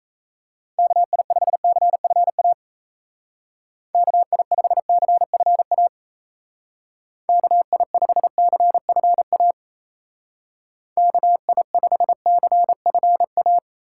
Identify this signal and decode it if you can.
Morse code, heard as a keyed tone, about 34 wpm, 700 Hz. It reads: KI5CFA KI5CFA KI5CFA KI5CFA
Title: Speed Racing - USA 2x3 Call Signs 28wpm